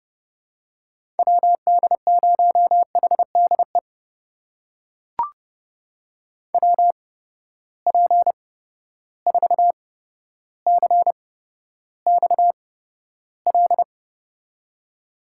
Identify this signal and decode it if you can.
Morse code, heard as a keyed tone, about 32 wpm, spaced as for 13 wpm, 700 Hz. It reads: WD0HDE WP4CXL
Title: USA Call Signs - 6 Characters - 8x Character Spacing - 30wpm